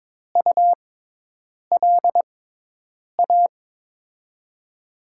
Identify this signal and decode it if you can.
Morse code, heard as a keyed tone, about 22 wpm, spaced as for 8 wpm, 700 Hz. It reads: ULA
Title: USA Call Signs - 6 Characters - 6x Character Spacing - 22wpm